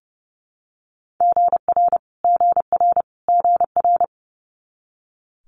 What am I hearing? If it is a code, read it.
Morse code, heard as a keyed tone, about 30 wpm, 700 Hz. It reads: GR GR GR
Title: Multi-letter Phonemes Repeated 3x 30wpm